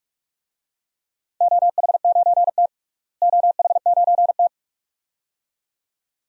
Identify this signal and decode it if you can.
Morse code, heard as a keyed tone, about 46 wpm, 700 Hz. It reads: OH9T OH9T
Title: DXCC-100 Call Signs - 4 Characters - Repeated 2x - 45wpm